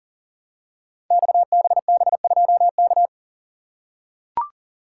Morse code, heard as a keyed tone, about 38 wpm, 700 Hz. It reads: XBB2K
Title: Code Groups - 5 Letter:Numbers - 40wpm